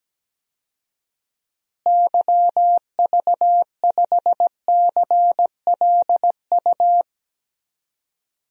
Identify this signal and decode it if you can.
Morse code, heard as a keyed tone, about 17 wpm, 700 Hz. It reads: YV5CLU
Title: DXCC-100 Call Signs - 6 Characters - Repeated 2x - 17wpm